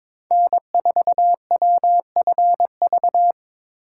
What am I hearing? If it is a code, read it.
Morse code, heard as a keyed tone, about 22 wpm, 700 Hz. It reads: N4WFV
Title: USA Call Signs - 4x Character Spacing - 22wpm